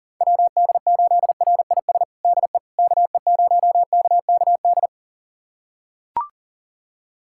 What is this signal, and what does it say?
Morse code, heard as a keyed tone, about 37 wpm, 700 Hz. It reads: WD8RIS DE KE0KKD
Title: USA Callsign DE Callsign - 6 Characters - 40wpm